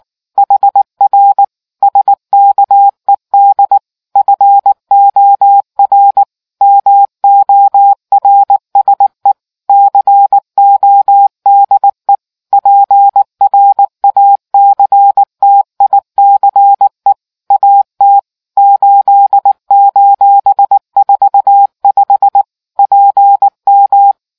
Morse code, heard as a keyed tone, about 19 wpm, 800 Hz. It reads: HR SKED FOR MORSE CODE PRACTICE AT 8:45 PM